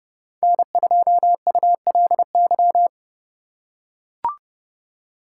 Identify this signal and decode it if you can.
Morse code, heard as a keyed tone, about 29 wpm, 700 Hz. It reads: N2ULY
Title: USA Call Signs - 5 Characters - 2x Character Spacing - 30wpm